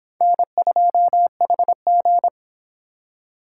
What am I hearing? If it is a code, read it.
Morse code, heard as a keyed tone, about 25 wpm, 700 Hz. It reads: N2HG